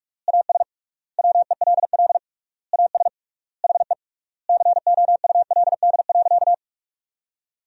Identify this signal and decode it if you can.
Morse code, heard as a keyed tone, about 46 wpm, 700 Hz. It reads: AS WELL AS HE COULD.
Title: 3x Word-Spacing - Sentences from Top 500 Words - 45wpm